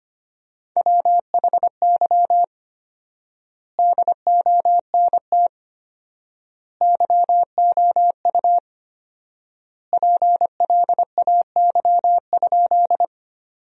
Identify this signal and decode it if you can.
Morse code, heard as a keyed tone, about 25 wpm, 700 Hz. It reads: WHY DONT YOU PLAY?
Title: 4x Word-Spacing - Sentences from Top 300 Words - 25wpm